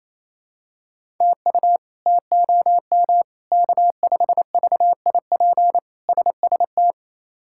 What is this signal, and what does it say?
Morse code, heard as a keyed tone, about 28 wpm, 700 Hz. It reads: TU TOM K5VIP SST
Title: K1USN SST QSO Practice - 28wpm